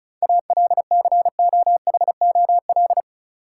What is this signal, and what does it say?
Morse code, heard as a keyed tone, about 36 wpm, 700 Hz. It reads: ALCOHOL
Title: Sentences from Top 1400 Words - New Word Drill - 35wpm